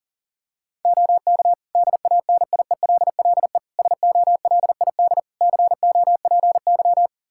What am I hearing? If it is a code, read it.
Morse code, heard as a keyed tone, about 40 wpm, 700 Hz. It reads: OK DANIELLE SOLID COPY
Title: Practice QSO - Exchange 3 - More Info - 40wpm